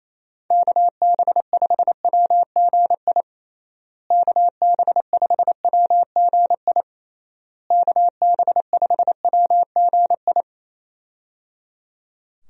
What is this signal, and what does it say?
Morse code, heard as a keyed tone, about 28 wpm, 700 Hz. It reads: KB5WGI KB5WGI KB5WGI
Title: USA Call Signs - 6 Characters - Repeated 3x - 28wpm